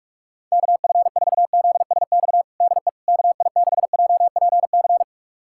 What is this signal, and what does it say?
Morse code, heard as a keyed tone, about 46 wpm, 690 Hz. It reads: KF4ZSX DE KI6JPC
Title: USA Callsign DE Callsign 45wpm